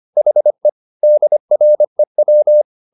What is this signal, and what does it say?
Morse code, heard as a keyed tone, about 25 wpm, 600 Hz. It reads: HE DREW
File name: the old man and the sea part 1 25wpm